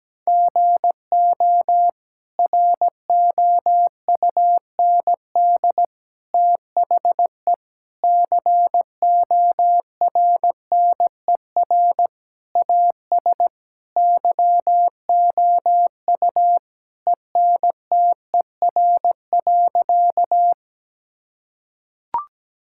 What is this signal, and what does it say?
Morse code, heard as a keyed tone, about 17 wpm, 700 Hz. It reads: GO ROUND THE CORNER AS YOU ENTER.
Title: Sentences from Top 1600 Words 17wpm